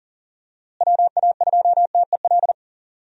Morse code, heard as a keyed tone, about 40 wpm, 700 Hz. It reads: WA2TEL
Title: USA Call Signs - 6 Characters - Repeated 2x - 40wpm